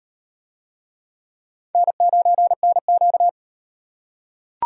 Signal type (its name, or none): Morse code